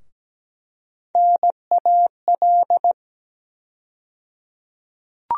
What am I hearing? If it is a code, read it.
Morse code, heard as a keyed tone, about 17 wpm, 700 Hz. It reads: NAL
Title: USA Call Sign Postfixes - 3 Letters 17wpm